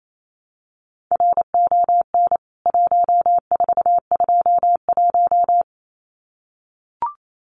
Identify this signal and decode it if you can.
Morse code, heard as a keyed tone, about 28 wpm, 700 Hz. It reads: RON 1421
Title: CWT Exchange 28wpm